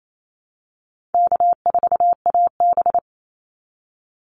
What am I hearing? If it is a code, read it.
Morse code, heard as a keyed tone, about 28 wpm, 700 Hz. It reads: K4AB
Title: Call Signs Repeated 2x - 28wpm